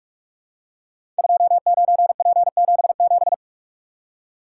Morse code, heard as a keyed tone, about 44 wpm, 700 Hz. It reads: 29P77
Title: Code Groups - 5 Letter:Numbers - 45wpm